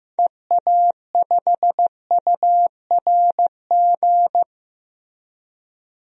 Morse code, heard as a keyed tone, about 15 wpm, 700 Hz. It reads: EA5URG
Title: DXCC-100 Call Signs - 6 Characters - Repeated 3x - 15wpm